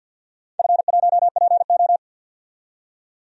Morse code, heard as a keyed tone, about 47 wpm, 700 Hz. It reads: F1PO
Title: DXCC-100 Call Signs - 4 Characters - 50wpm